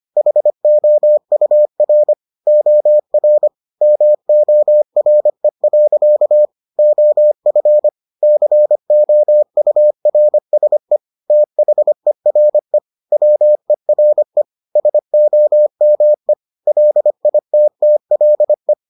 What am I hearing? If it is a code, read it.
Morse code, heard as a keyed tone, about 25 wpm, 600 Hz. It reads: HOUR OR MORE. OF COURSE THERE WERE SOME LITTLE